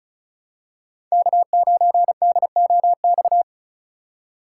Morse code, heard as a keyed tone, about 36 wpm, 700 Hz. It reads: K9DOX
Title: USA Call Signs - 2x Character Spacing - 35wpm